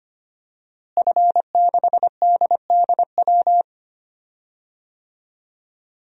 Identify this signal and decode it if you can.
Morse code, heard as a keyed tone, about 25 wpm, 700 Hz. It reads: F6DDW
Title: DXCC-100 Call Signs - 5 Characters - Repeated 2x - 25wpm